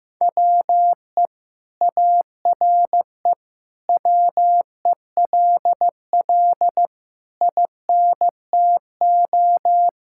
Morse code, heard as a keyed tone, about 15 wpm, 700 Hz. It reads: WE ARE WELL INTO
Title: Sentences from Top 1000 Words - Review Part 5 - 15wpm